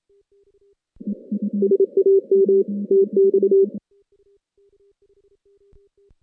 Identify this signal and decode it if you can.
Morse code, heard as a keyed tone, about 30 wpm, 410 Hz. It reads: SAM TX
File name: 2016 JAN CW NAQP / 2016 Jan naqp k5ot - 0-8